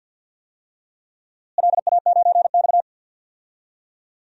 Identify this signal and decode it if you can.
Morse code, heard as a keyed tone, about 52 wpm, 700 Hz. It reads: LA9X
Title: DXCC-100 Call Signs - 4 Characters - Repeated 2x - 50wpm